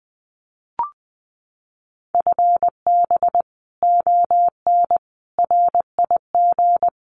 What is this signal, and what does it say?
Morse code, heard as a keyed tone, about 20 wpm, 700 Hz. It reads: FB ON RIG
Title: Common QSO Phrases Encore 20wpm